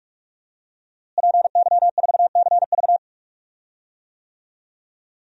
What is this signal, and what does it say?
Morse code, heard as a keyed tone, about 45 wpm, 700 Hz. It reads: PY4CV
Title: DXCC-100 Call Signs - 5 Characters - Repeated 2x - 45wpm